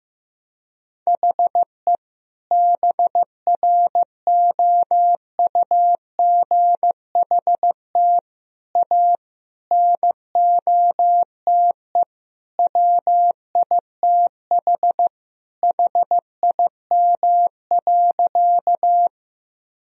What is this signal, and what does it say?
Morse code, heard as a keyed tone, about 16 wpm, 700 Hz. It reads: HE BROUGHT A NOTE WITH HIM.
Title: Sentences from Top 1000 Words - Review Part 2 - 15wpm